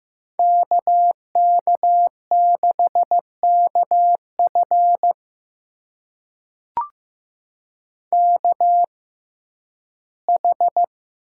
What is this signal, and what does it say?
Morse code, heard as a keyed tone, about 15 wpm, spaced as for 11 wpm, 700 Hz. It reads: KK6KF KH
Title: USA Call Signs - 6x Character Spacing - 15wpm